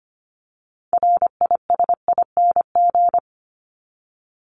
Morse code, heard as a keyed tone, about 25 wpm, 700 Hz. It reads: RISING
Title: ING Suffix - Top 100 Words with Base Words 25wpm